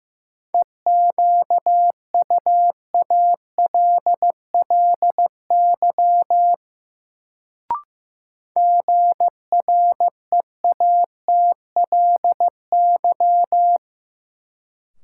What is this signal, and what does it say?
Morse code, heard as a keyed tone, about 15 wpm, 700 Hz. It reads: EQUALLY GREATLY
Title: Sentences from Top 1400 Words - New Word Drill - 15wpm